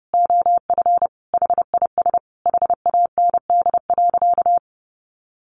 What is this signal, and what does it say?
Morse code, heard as a keyed tone, about 30 wpm, 700 Hz. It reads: OF HIS HAND.
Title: Sentences from Top 1400 Words 30wpm